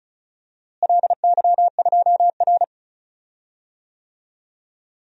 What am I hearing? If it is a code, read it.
Morse code, heard as a keyed tone, about 36 wpm, 700 Hz. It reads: LY2R
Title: DXCC-100 Call Signs - 4 Characters - Repeated 2x - 35wpm